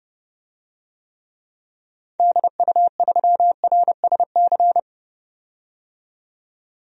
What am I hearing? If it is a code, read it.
Morse code, heard as a keyed tone, about 30 wpm, 700 Hz. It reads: DU3RSC
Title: DXCC-100 Call Signs - 6 Characters - 30wpm